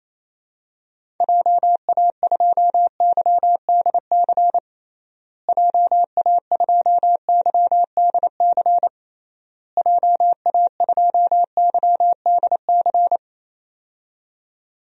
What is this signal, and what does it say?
Morse code, heard as a keyed tone, about 27 wpm, 700 Hz. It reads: JA2YDC JA2YDC JA2YDC
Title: DXCC-100 Call Signs - 6 Characters - Repeated 3x - 28wpm